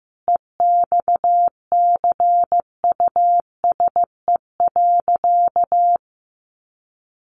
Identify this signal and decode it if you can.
Morse code, heard as a keyed tone, about 15 wpm, 700 Hz. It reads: EXCUSE.
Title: Sentences from Top 800 Words 15wpm